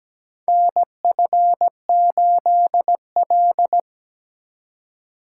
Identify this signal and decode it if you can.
Morse code, heard as a keyed tone, about 17 wpm, 700 Hz. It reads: NF8L
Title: USA Call Signs - 4 Characters - 3x Character Spacing - 17wpm